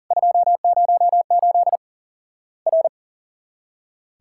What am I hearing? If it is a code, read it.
Morse code, heard as a keyed tone, about 40 wpm, 700 Hz. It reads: 208 R
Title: Worked All Europe DX Contest - QTC Exchange - 40wpm